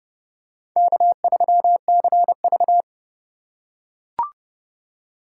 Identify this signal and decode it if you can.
Morse code, heard as a keyed tone, about 31 wpm, 700 Hz. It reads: K3CV
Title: USA Call Signs - 4 Characters - 4x Character Spacing - 30wpm